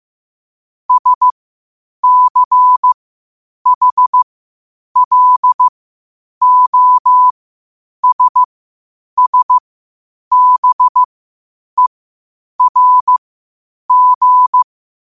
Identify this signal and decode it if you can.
Morse code, heard as a keyed tone, about 15 wpm, spaced as for 8 wpm, 1000 Hz. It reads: SCHLOSSBERG